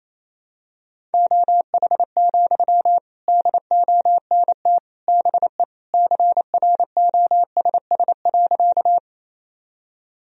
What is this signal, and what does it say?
Morse code, heard as a keyed tone, about 28 wpm, 700 Hz. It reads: OH, DONT BE CROSS.
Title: Sentences from Top 1000 Words - Review Part 1 - 28wpm